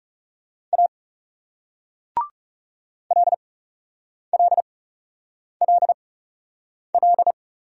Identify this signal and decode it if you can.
Morse code, heard as a keyed tone, about 40 wpm, spaced as for 8 wpm, 700 Hz. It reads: A LLLL